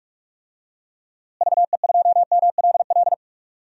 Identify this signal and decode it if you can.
Morse code, heard as a keyed tone, about 46 wpm, 700 Hz. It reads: VE2MLL
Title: DXCC-100 Call Signs - 6 Characters - Repeated 2x - 45wpm